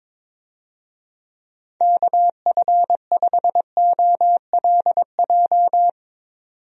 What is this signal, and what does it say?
Morse code, heard as a keyed tone, about 22 wpm, 700 Hz. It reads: KF5OLJ
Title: USA Call Signs - 6 Characters - 22wpm